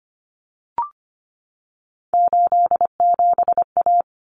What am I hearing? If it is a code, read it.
Morse code, heard as a keyed tone, about 25 wpm, 700 Hz. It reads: 87A
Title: Common Cut Numbers - 3-Digit 25wpm